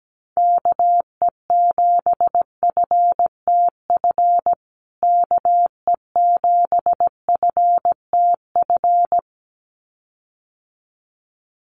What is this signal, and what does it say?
Morse code, heard as a keyed tone, about 17 wpm, 700 Hz. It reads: KE7FTF KE7FTF
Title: USA Call Signs Repeated 2x 17wpm